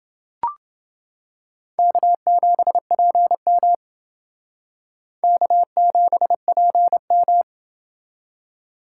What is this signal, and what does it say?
Morse code, heard as a keyed tone, about 31 wpm, 700 Hz. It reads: K7PM K7PM